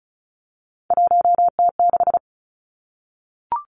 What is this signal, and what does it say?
Morse code, heard as a keyed tone, about 36 wpm, 700 Hz. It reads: 1T6
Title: Three Challenging Letter-Number Combinations 35wpm